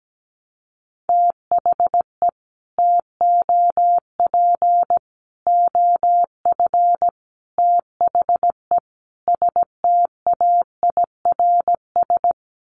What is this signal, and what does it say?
Morse code, heard as a keyed tone, about 17 wpm, 700 Hz. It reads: THE TOP OF THE STAIRS
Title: Sets of 5 Words Encore 17wpm